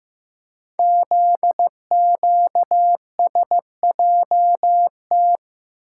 Morse code, heard as a keyed tone, about 15 wpm, 700 Hz. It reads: ZQSJT
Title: Code Groups - 5 Letters - 15wpm